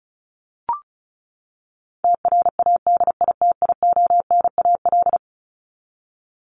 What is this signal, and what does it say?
Morse code, heard as a keyed tone, about 34 wpm, 700 Hz. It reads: TRADITIONAL
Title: Top 751-1000 Words 35wpm